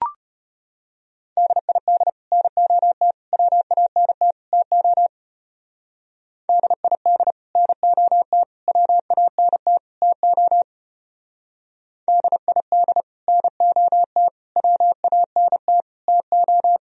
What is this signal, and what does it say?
Morse code, heard as a keyed tone, about 31 wpm, 700 Hz. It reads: DID NOT WANT TO DID NOT WANT TO DID NOT WANT TO
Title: Speed-Racing - Sets of 4 Words 25wpm